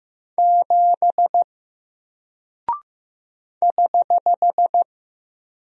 Morse code, heard as a keyed tone, about 15 wpm, 700 Hz. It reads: 7 <HH>
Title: Single Character (Letter:Number:Punctuation:Prosign) Alt 15wpm